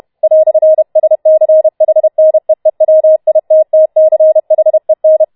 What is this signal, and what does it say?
Morse code, heard as a keyed tone, about 31 wpm, 630 Hz. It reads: "SCHNEEWITTCHEN